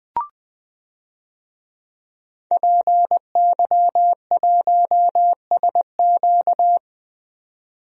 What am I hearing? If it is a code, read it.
Morse code, heard as a keyed tone, about 20 wpm, 700 Hz. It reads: PY1SQ
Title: DXCC-100 Call Signs - 5 Characters - 20wpm